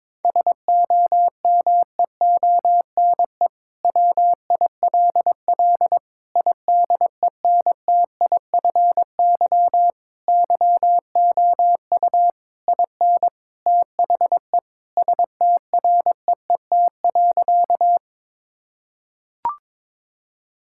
Morse code, heard as a keyed tone, about 22 wpm, 700 Hz. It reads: SOMEONE WILL IDENTIFY YOU IN THE STREET.
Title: Sentences from Top 1000 Words - Review Part 6 - 22wpm